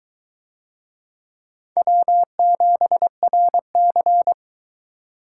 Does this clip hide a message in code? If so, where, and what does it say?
Yes: Morse code from 1.8 to 4.3 s, heard as a keyed tone, about 21 wpm, 700 Hz. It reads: W7RC